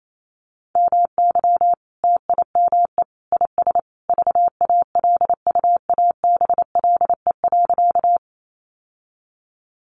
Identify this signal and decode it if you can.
Morse code, heard as a keyed tone, about 28 wpm, 700 Hz. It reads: MY TIME IS VALUABLE.
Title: Sentences from Top 1100 Words 28wpm